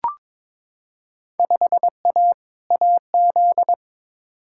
Morse code, heard as a keyed tone, about 21 wpm, 700 Hz. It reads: 5A AZ
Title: Field Day Exchanges - 22wpm